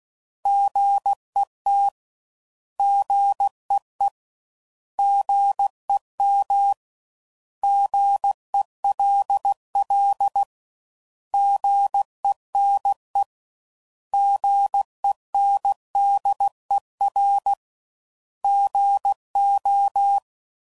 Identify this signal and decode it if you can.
Morse code, heard as a keyed tone, about 16 wpm, 800 Hz. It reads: GET GEE GEM GELL GENE GENDER GO
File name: Prefixes / By First Letter / Gs